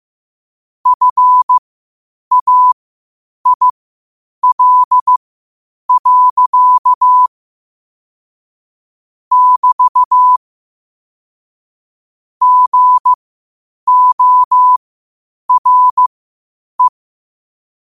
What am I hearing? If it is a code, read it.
Morse code, heard as a keyed tone, about 15 wpm, spaced as for 8 wpm, 1000 Hz. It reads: FAIL. <BT> GORE